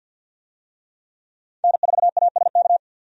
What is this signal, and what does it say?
Morse code, heard as a keyed tone, about 51 wpm, 700 Hz. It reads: N4ASK
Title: USA Call Signs - 5 Characters - 50wpm